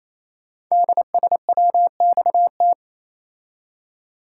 Morse code, heard as a keyed tone, about 29 wpm, 700 Hz. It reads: DSWXT
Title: Code Groups - 5 Letters - 28wpm